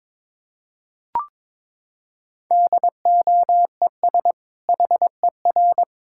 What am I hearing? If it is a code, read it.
Morse code, heard as a keyed tone, about 22 wpm, 700 Hz. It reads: DOES HER
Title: Sentences from Top 1000 Words - Review Part 5 - 22wpm